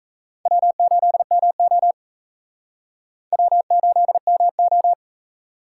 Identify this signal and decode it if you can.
Morse code, heard as a keyed tone, about 36 wpm, 700 Hz. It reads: W8MO W8MO